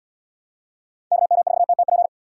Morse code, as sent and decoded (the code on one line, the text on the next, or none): -.- -- ....- - - .--.
KM4TTP